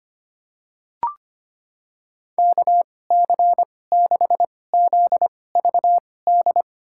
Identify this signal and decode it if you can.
Morse code, heard as a keyed tone, about 25 wpm, spaced as for 18 wpm, 700 Hz. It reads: KC6ZVD
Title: USA Call Signs - 6 Characters - 2x Character Spacing - 25wpm